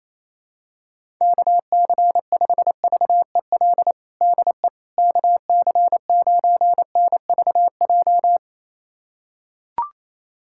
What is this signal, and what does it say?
Morse code, heard as a keyed tone, about 27 wpm, 700 Hz. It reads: KC5VEL DE KC9NVJ
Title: USA Callsign DE Callsign - 6 Characters - 28wpm